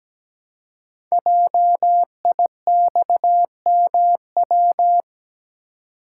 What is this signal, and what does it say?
Morse code, heard as a keyed tone, about 17 wpm, 700 Hz. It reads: JIXMW
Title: Code Groups - 5 Letters - 17wpm